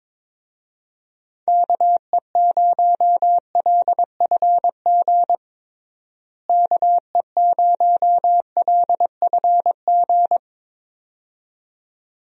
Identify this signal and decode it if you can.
Morse code, heard as a keyed tone, about 22 wpm, 700 Hz. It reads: KE0LFG KE0LFG
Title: USA Call Signs - 6 Characters - Repeated 2x - 22wpm